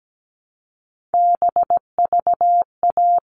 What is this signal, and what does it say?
Morse code, heard as a keyed tone, about 17 wpm, 700 Hz. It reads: BVA
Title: All Cut Numbers - 3-Digit 17wpm